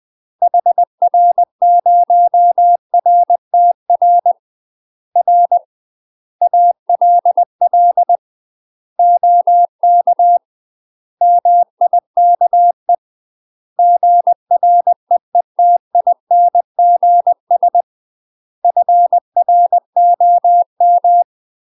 Morse code, heard as a keyed tone, about 20 wpm, 700 Hz. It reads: HR0RTR R ALL OK MIKE GREETINGS FROM
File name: UK MASTER QSOs TRAINING / 20WPM / UK MASTER QSOs 0011